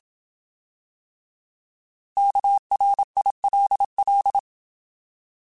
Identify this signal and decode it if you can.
Morse code, heard as a keyed tone, about 27 wpm, 800 Hz. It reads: KRILL
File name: Alphabetical / Alphabetical mp3s / Ks @25